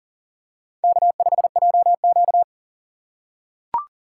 Morse code, heard as a keyed tone, about 39 wpm, 700 Hz. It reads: K5JQ
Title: USA Call Signs - 4 Characters - 4x Character Spacing - 40wpm